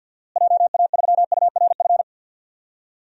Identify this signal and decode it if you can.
Morse code, heard as a keyed tone, about 52 wpm, 700 Hz. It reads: JA3URF
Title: DXCC-100 Call Signs - 6 Characters - Repeated 2x - 50wpm